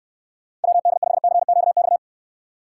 Morse code, heard as a keyed tone, about 69 wpm, 700 Hz. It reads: WD5CQX